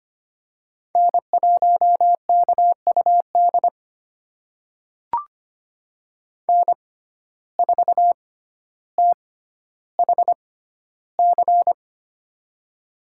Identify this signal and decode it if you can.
Morse code, heard as a keyed tone, about 25 wpm, spaced as for 13 wpm, 700 Hz. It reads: N1KUD N4THC
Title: USA Call Signs - 5 Characters - 6x Character Spacing - 25wpm